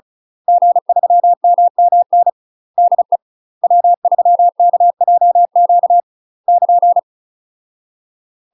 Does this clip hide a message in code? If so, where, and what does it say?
Yes: Morse code from 0.5 to 7.0 s, heard as a keyed tone, about 36 wpm, 700 Hz. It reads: G3MMN DE W3KJQ <KN>